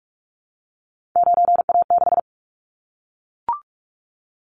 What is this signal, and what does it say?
Morse code, heard as a keyed tone, about 44 wpm, 700 Hz. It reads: 9A6
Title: Callsign Prefixes for Top 100 DXCC - 45wpm